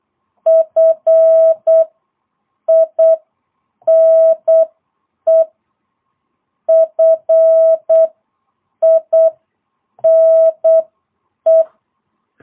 Morse code, heard as a keyed tone, about 8 wpm, spaced as for 6 wpm, 650 Hz. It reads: FINE FINE
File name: jh8cbh / 2022cw